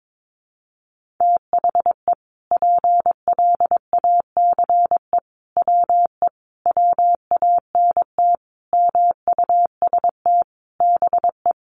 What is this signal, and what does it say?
Morse code, heard as a keyed tone, about 22 wpm, 700 Hz. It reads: THE PLACE WE WANT MUST BE
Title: Sentences from Top 1400 Words 22wpm